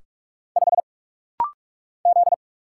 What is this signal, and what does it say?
Morse code, heard as a keyed tone, about 44 wpm, 700 Hz. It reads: 5 Z
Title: Single Letter-Number - Rapid-Fire 45wpm